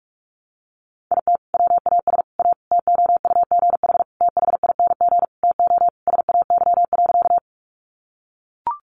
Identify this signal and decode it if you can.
Morse code, heard as a keyed tone, about 44 wpm, 700 Hz. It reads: IT WAS A TOUGH THING TO SAY.
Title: Sentences from Top 1200 Words 45wpm